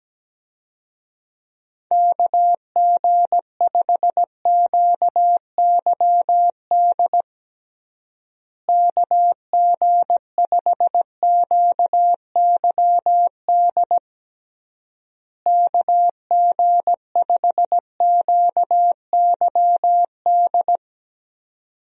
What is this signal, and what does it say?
Morse code, heard as a keyed tone, about 17 wpm, 700 Hz. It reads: KG5QYD KG5QYD KG5QYD
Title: USA Call Signs - 6 Characters - Repeated 3x - 17wpm